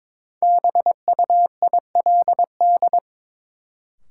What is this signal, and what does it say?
Morse code, heard as a keyed tone, about 22 wpm, 700 Hz. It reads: BUILD
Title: Sentences from Top 1100 Words - New Word Drill Alt - 22wpm